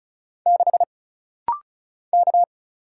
Morse code, heard as a keyed tone, about 36 wpm, 700 Hz. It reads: 6 K
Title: Single Letter-Number - Rapid-Fire 35wpm